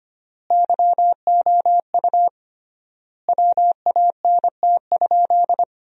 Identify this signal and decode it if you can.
Morse code, heard as a keyed tone, about 25 wpm, 700 Hz. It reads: YOU WANT?
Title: 3x Word-Spacing - Sentences from Top 200 Words - 25wpm